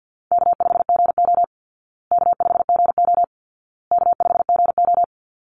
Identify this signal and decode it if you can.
Morse code, heard as a keyed tone, about 50 wpm, 700 Hz. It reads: K5GO K5GO K5GO
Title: Call Signs Repeated 2x - 50wpm